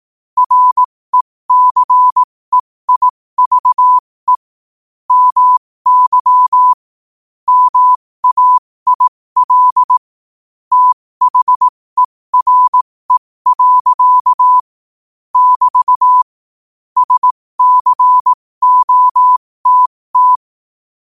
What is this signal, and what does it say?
Morse code, heard as a keyed tone, about 18 wpm, spaced as for 15 wpm, 1000 Hz. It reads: RECEIVE MY MAIL THERE. <BT> SCOTT
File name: qotdfiles / 15wpm